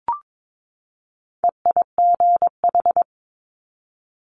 Morse code, heard as a keyed tone, about 22 wpm, 700 Hz. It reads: EIGH
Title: Multi-letter Phonemes 22wpm